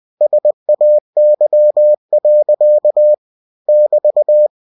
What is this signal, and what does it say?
Morse code, heard as a keyed tone, about 20 wpm, 600 Hz. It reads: SAY. <BT>
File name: cw-books / winnie-the-pooh / winnie-20wpm-page-0079